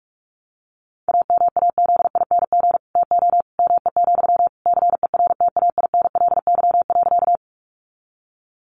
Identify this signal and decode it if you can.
Morse code, heard as a keyed tone, about 44 wpm, 700 Hz. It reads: AMAZING TO ME, CERTAINLY.
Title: Sentences from Top 1500 Words 45wpm